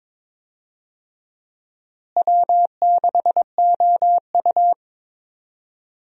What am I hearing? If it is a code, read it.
Morse code, heard as a keyed tone, about 22 wpm, 700 Hz. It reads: W6OU
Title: USA Call Signs - 4 Characters - Repeated 3x - 22wpm